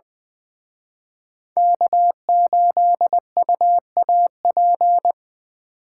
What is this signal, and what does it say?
Morse code, heard as a keyed tone, about 24 wpm, 700 Hz. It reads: K8UAP
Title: Speed Racing - USA 1x3 Call Signs 20wpm